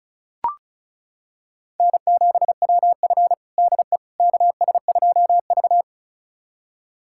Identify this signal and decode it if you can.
Morse code, heard as a keyed tone, about 36 wpm, 700 Hz. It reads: N7WF DE KS2V
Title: USA Callsign DE Callsign - 4 Characters - 35wpm